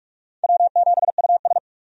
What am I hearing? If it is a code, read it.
Morse code, heard as a keyed tone, about 46 wpm, 700 Hz. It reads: W7US
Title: USA Call Signs - 4 Characters - 45wpm